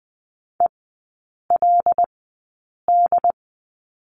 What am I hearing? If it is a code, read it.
Morse code, heard as a keyed tone, about 20 wpm, spaced as for 8 wpm, 700 Hz. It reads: ELD
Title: 100 Lesson 17e - Letter u - Three Characters 20wpm